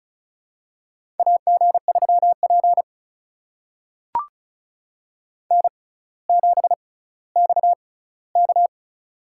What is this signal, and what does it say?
Morse code, heard as a keyed tone, about 36 wpm, spaced as for 18 wpm, 700 Hz. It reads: AG3P N7XK
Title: USA Call Signs - 4 Characters - 6x Character Spacing - 35wpm